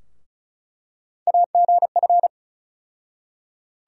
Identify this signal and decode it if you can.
Morse code, heard as a keyed tone, about 36 wpm, 700 Hz. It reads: AGF
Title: USA Call Sign Postfixes - 3 Letters 35wpm